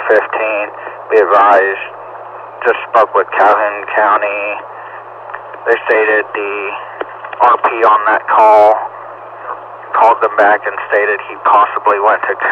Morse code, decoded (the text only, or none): none